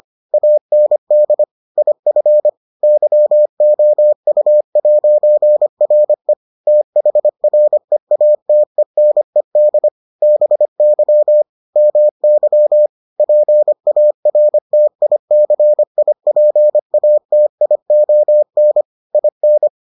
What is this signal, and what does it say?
Morse code, heard as a keyed tone, about 25 wpm, 600 Hz. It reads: AND IF YOU'RE THREATENED BY MY PARTICIPATION IN